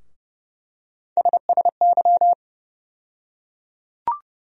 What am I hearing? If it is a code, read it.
Morse code, heard as a keyed tone, about 30 wpm, 700 Hz. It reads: SSY
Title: USA Call Sign Postfixes - 3 Letters 30wpm